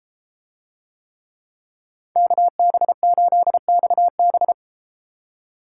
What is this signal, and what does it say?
Morse code, heard as a keyed tone, about 32 wpm, 700 Hz. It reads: KB8XB